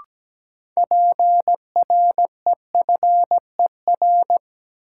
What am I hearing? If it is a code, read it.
Morse code, heard as a keyed tone, about 17 wpm, 700 Hz. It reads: PREFER
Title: Sentences from Top 1100 Words - New Word Drill - 17wpm